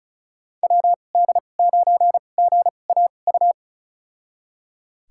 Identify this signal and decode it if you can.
Morse code, heard as a keyed tone, about 36 wpm, spaced as for 25 wpm, 700 Hz. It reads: WD9GAU
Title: USA Call Signs - 6 Characters - 2x Character Spacing - 35wpm